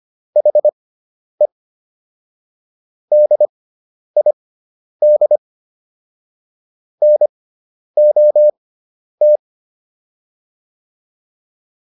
Morse code, heard as a keyed tone, about 25 wpm, spaced as for 10 wpm, 600 Hz. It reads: HE DID NOT